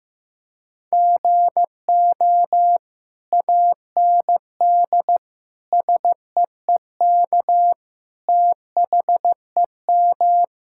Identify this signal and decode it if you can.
Morse code, heard as a keyed tone, about 15 wpm, 700 Hz. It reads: GO AND SEEK THEM
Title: Sentences from Top 1600 Words 15wpm